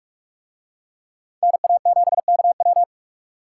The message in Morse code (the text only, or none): NA7KW